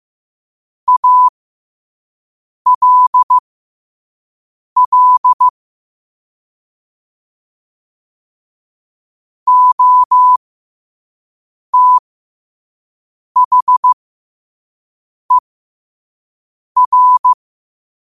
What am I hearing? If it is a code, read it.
Morse code, heard as a keyed tone, about 15 wpm, spaced as for 5 wpm, 1000 Hz. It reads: ALL OTHER
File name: 5wpm